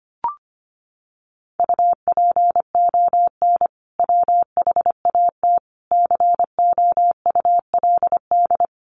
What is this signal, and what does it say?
Morse code, heard as a keyed tone, about 25 wpm, 700 Hz. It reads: UPON WHAT COULD
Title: Sentences from Top 1500 Words 25wpm